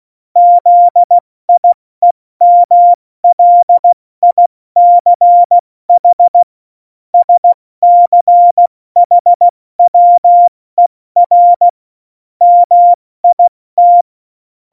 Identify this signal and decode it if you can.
Morse code, heard as a keyed tone, about 16 wpm, spaced as for 14 wpm, 700 Hz. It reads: ZIEMLICH SCHWER MIT
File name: Morsekurs / Witze / Witz 02. Oktober 2018, 70 F80